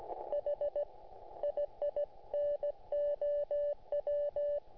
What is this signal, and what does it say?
Morse code, heard as a keyed tone, about 16 wpm, spaced as for 14 wpm, 580 Hz. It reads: HI I NOW